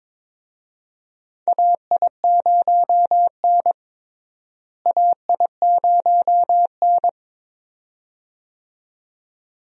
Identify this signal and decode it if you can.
Morse code, heard as a keyed tone, about 22 wpm, 700 Hz. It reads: AI0N AI0N
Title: USA Call Signs - 4 Characters - Repeated 2x - 22wpm